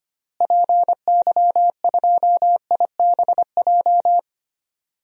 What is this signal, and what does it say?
Morse code, heard as a keyed tone, about 25 wpm, 700 Hz. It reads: PY2IBJ
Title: DXCC-100 Call Signs - 6 Characters - Repeated 3x - 25wpm